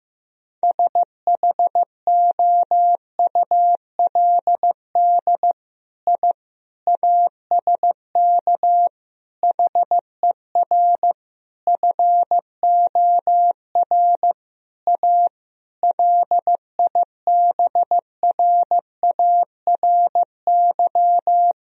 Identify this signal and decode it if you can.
Morse code, heard as a keyed tone, about 15 wpm, 700 Hz. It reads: SHOULD I ASK HER FOR A LIBRARY
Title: Sentences from Top 1000 Words - Review Part 4 - 15wpm